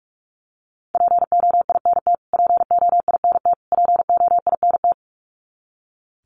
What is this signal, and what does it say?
Morse code, heard as a keyed tone, about 44 wpm, 700 Hz. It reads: POINT POINT POINT
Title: Top 200 Words Repeated 45wpm